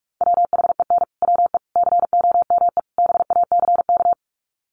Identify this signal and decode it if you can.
Morse code, heard as a keyed tone, about 44 wpm, 700 Hz. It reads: WHEN WE COME BACK
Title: Sets of 4 Words 45wpm